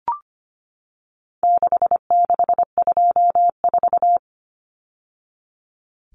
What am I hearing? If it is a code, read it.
Morse code, heard as a keyed tone, about 25 wpm, 700 Hz. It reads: 6624
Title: Numbers - 4-Digit 25wpm